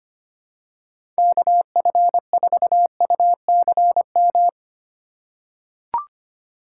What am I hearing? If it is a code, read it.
Morse code, heard as a keyed tone, about 25 wpm, 700 Hz. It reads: KF4UCM
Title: USA Call Signs - 6 Characters - 4x Character Spacing - 25wpm